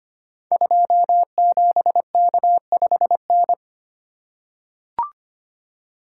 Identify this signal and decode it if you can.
Morse code, heard as a keyed tone, about 25 wpm, 700 Hz. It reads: 27K5N
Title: Code Groups - 5 Letter:Numbers - 25wpm